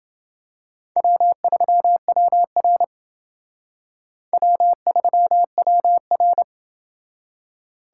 Morse code, heard as a keyed tone, about 29 wpm, 700 Hz. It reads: W3WR W3WR